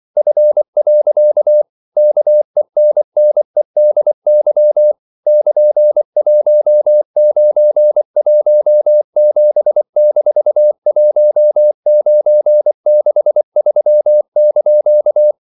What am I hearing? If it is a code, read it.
Morse code, heard as a keyed tone, about 24 wpm, 600 Hz. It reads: F. KENNEDY <KN>1917-1963)
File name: CW / Practice / EnglishQuotations0324